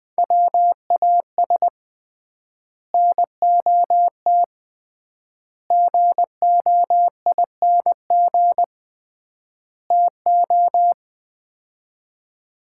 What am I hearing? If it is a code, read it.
Morse code, heard as a keyed tone, about 21 wpm, 700 Hz. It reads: WAS NOT GOING TO
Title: 3x Word-Spacing - Sets of 4 Words - 20wpm